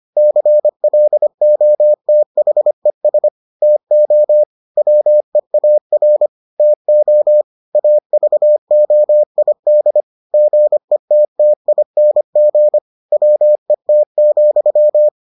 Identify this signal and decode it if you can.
Morse code, heard as a keyed tone, about 25 wpm, 600 Hz. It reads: CLOTHES TO WEAR TO AVOID GETTING WET,